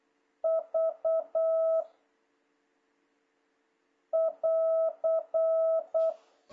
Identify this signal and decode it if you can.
Morse code, heard as a keyed tone, about 8 wpm, 660 Hz. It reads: V <AR>
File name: jh8cbh / 2026cw